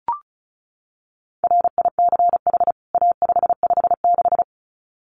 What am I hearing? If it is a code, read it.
Morse code, heard as a keyed tone, about 34 wpm, 700 Hz. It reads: RICH A556
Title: CWT Exchange 35wpm